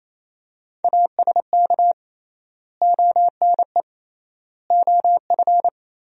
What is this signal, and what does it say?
Morse code, heard as a keyed tone, about 28 wpm, 700 Hz. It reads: ASK ONE OF
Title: 3x Word-Spacing - Sentences from Top 300 Words - 28wpm